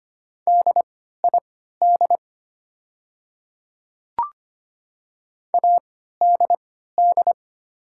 Morse code, heard as a keyed tone, about 25 wpm, spaced as for 14 wpm, 700 Hz. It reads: DID ADD
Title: 3x Character Spacing - Top 200 Words - 25wpm